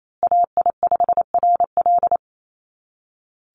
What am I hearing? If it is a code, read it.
Morse code, heard as a keyed tone, about 28 wpm, 700 Hz. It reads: AI5RL
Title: USA Call Signs Repeated 2x 28wpm